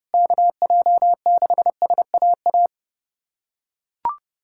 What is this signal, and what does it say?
Morse code, heard as a keyed tone, about 32 wpm, 700 Hz. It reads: KJ6SAA
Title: USA Callsign DE Callsign 30wpm